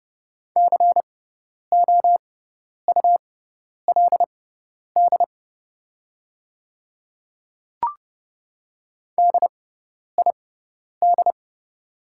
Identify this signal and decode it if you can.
Morse code, heard as a keyed tone, about 30 wpm, spaced as for 10 wpm, 690 Hz. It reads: COULD DID
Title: 6x Character Spacing - Top 300 Words - 30wpm